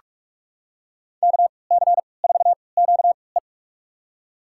Morse code, heard as a keyed tone, about 46 wpm, spaced as for 26 wpm, 700 Hz. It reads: KC4QE
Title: USA Call Signs - 5 Characters - 3x Character Spacing - 45wpm